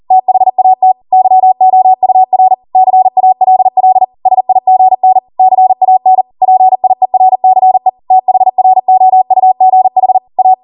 Morse code, heard as a keyed tone, about 40 wpm, 750 Hz. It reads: THAT YOUR CALL SIGN CAN PIERCE THROUGH A